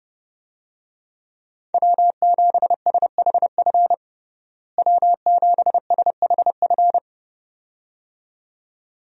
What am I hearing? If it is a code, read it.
Morse code, heard as a keyed tone, about 30 wpm, 700 Hz. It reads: W7SHF W7SHF
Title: USA Call Signs - 5 Characters - Repeated 2x - 30wpm